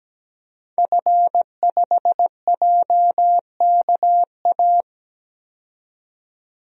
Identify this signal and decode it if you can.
Morse code, heard as a keyed tone, about 17 wpm, 700 Hz. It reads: F5JKA
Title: DXCC-100 Call Signs - 5 Characters - 17wpm